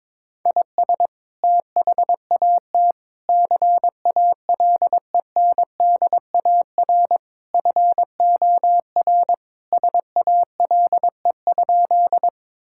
Morse code, heard as a keyed tone, about 22 wpm, 700 Hz. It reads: IS THAT CALENDAR FOR SALE?
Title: Sentences from Top 1600 Words 22wpm